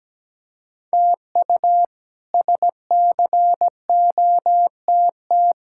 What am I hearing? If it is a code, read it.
Morse code, heard as a keyed tone, about 17 wpm, 700 Hz. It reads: TU SCOTT
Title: K1USN SST QSO Practice - 17wpm